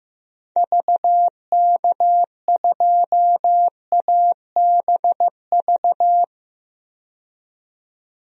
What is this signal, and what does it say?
Morse code, heard as a keyed tone, about 15 wpm, 700 Hz. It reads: VK2ABV
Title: DXCC-100 Call Signs - 6 Characters - 15wpm